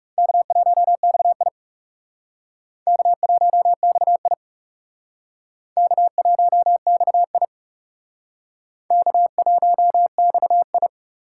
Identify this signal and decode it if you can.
Morse code, heard as a keyed tone, about 37 wpm, 700 Hz. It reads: K1XI K1XI K1XI K1XI
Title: Speed Racing - USA 1x2 Call Signs 30wpm